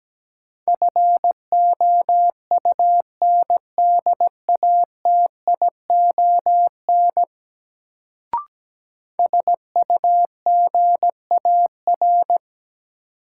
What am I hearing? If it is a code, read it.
Morse code, heard as a keyed tone, about 17 wpm, 700 Hz. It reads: FOUNDATION SUGAR
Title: Sentences from Top 1200 Words - New Word Drill - 17wpm